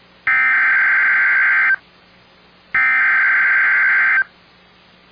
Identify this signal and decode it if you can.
Emergency Alert System (SAME) tones (header bursts), heard as harsh buzzing bursts. It is heard from 0.3 to 4.2 s.